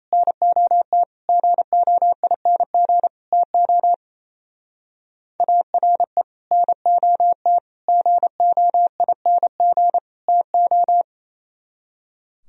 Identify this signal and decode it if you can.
Morse code, heard as a keyed tone, about 34 wpm, 700 Hz. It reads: NOT GOING TO ARE NOT GOING TO
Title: Speed-Racing - Sets of 4 Words 28wpm